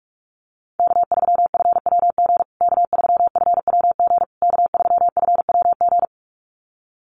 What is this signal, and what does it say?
Morse code, heard as a keyed tone, about 44 wpm, 700 Hz. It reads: K3FWG K3FWG K3FWG
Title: USA Call Signs Repeated 3x 45wpm